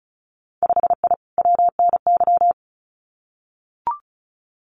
Morse code, heard as a keyed tone, about 36 wpm, 700 Hz. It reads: 5I WNY
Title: Winter Field Day Exchanges Encore 35wpm